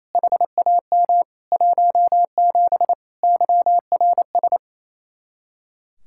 Morse code, heard as a keyed tone, about 35 wpm, 700 Hz. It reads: HAM 17 YRS